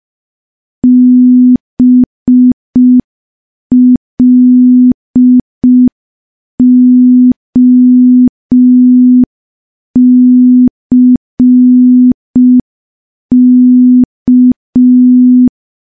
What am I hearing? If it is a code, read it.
Morse code, heard as a keyed tone, about 5 wpm, 250 Hz. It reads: BLOCK
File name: morse / morse-training-05WPM